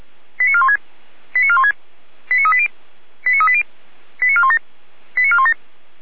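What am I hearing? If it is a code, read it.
Five-tone selective call (ZVEI), heard as quick runs of five tones, tones of about 70 ms. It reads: 89416 89416 89380 89380 89417 89417
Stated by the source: ZVEI five-tone selective call received with an SDR.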